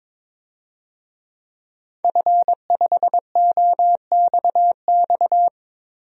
Morse code, heard as a keyed tone, about 22 wpm, 700 Hz. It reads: F5OXX
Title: DXCC-100 Call Signs - 5 Characters - Repeated 2x - 22wpm